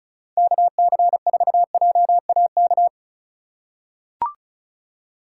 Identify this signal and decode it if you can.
Morse code, heard as a keyed tone, about 36 wpm, 700 Hz. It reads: KC4JAK
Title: USA Call Signs - 6 Characters - 4x Character Spacing - 35wpm